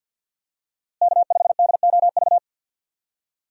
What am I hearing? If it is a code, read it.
Morse code, heard as a keyed tone, about 52 wpm, 700 Hz. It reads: K5DOV
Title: USA Call Signs - 5 Characters - Repeated 3x - 50wpm